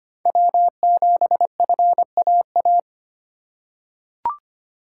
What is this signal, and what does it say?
Morse code, heard as a keyed tone, about 25 wpm, 700 Hz. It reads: W7FAA
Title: USA Call Signs - 5 Characters - 2x Character Spacing - 25wpm